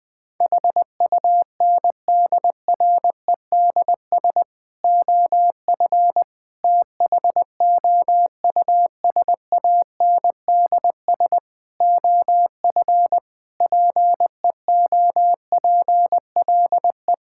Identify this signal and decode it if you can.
Morse code, heard as a keyed tone, about 21 wpm, 700 Hz. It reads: HUNDREDS OF THOUSANDS OF PEOPLE
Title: Sentences with 4-Word Phrases - 20wpm